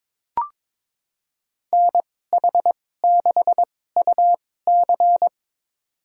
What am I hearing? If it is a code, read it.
Morse code, heard as a keyed tone, about 22 wpm, spaced as for 16 wpm, 700 Hz. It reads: NH6UC
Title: USA Call Signs - 5 Characters - 2x Character Spacing - 22wpm